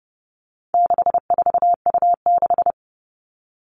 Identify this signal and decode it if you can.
Morse code, heard as a keyed tone, about 30 wpm, 700 Hz. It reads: 64U6
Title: All Cut Numbers - 4-Digit 30wpm